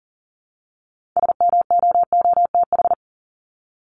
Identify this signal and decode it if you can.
Morse code, heard as a keyed tone, about 40 wpm, 700 Hz. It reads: SMOOTH
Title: Top 1251-1500 Words 40wpm